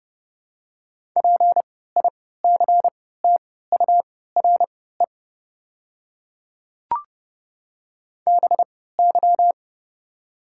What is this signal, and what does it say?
Morse code, heard as a keyed tone, about 29 wpm, spaced as for 17 wpm, 690 Hz. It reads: PICTURE BY
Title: 3x Character Spacing - Top 200 Words - 30wpm